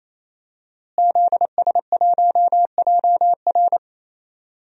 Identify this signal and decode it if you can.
Morse code, heard as a keyed tone, about 27 wpm, 700 Hz. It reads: ZS1JR
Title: DXCC-100 Call Signs - 5 Characters - Repeated 3x - 28wpm